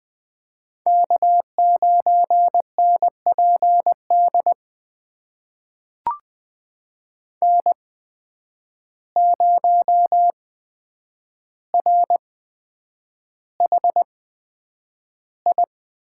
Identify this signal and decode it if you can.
Morse code, heard as a keyed tone, about 20 wpm, spaced as for 9 wpm, 700 Hz. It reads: K9NPD N0RHI
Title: USA Call Signs - 5 Characters - 8x Character Spacing - 20wpm